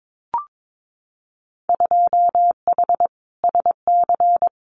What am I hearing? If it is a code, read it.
Morse code, heard as a keyed tone, about 22 wpm, 700 Hz. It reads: 2H SC
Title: Winter Field Day Exchanges 22wpm